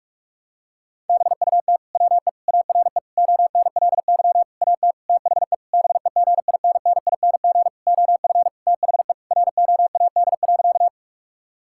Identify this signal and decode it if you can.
Morse code, heard as a keyed tone, about 46 wpm, 700 Hz. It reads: BUT WE ARE ONLY AT THE BEGINNING OF THE ROAD.
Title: Sentences from Top 1000 Words - Review Part 5 - 45wpm